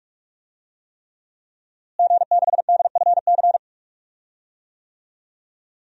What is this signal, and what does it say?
Morse code, heard as a keyed tone, about 46 wpm, 700 Hz. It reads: G6DFC
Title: DXCC-100 Call Signs - 5 Characters - Repeated 3x - 45wpm